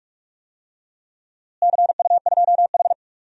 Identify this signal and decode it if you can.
Morse code, heard as a keyed tone, about 46 wpm, 700 Hz. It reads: CU2H
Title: DXCC-100 Call Signs - 4 Characters - Repeated 3x - 45wpm